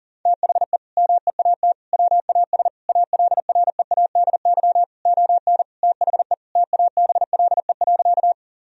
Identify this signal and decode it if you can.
Morse code, heard as a keyed tone, about 40 wpm, 700 Hz. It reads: THE MEAT WAS ALREADY ON THE TABLE.
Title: Sentences from Top 1000 Words - Review Part 3 - 40wpm